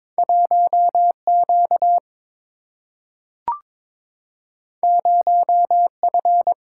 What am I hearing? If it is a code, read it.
Morse code, heard as a keyed tone, about 22 wpm, 700 Hz. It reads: 1Q 0F
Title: Single-Letter Single-Number 22wpm